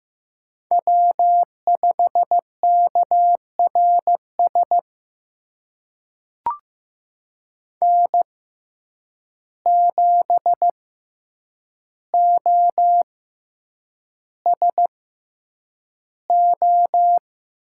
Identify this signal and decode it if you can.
Morse code, heard as a keyed tone, about 15 wpm, spaced as for 8 wpm, 700 Hz. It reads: W5KRS N7OSO
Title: USA Call Signs - 5 Characters - 6x Character Spacing - 15wpm